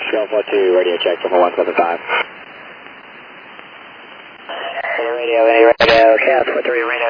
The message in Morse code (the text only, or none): none